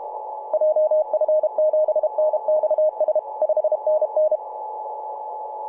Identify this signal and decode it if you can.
Morse code, heard as a keyed tone, about 32 wpm, 620 Hz, with radio noise. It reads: JF7NXS 5NN
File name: vk9-g7vjr-24-903mhz-cw